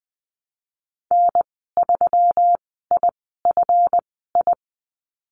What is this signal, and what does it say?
Morse code, heard as a keyed tone, about 20 wpm, spaced as for 15 wpm, 700 Hz. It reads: N3IFI
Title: 179 Lesson 29g - Slash - Call Signs 20wpm